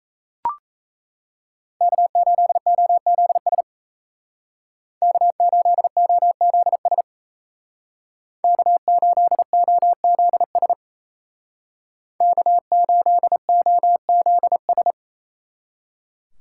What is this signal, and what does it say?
Morse code, heard as a keyed tone, about 34 wpm, 700 Hz. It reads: K8OZS K8OZS K8OZS K8OZS
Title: Speed Racing - USA Call Signs 28wpm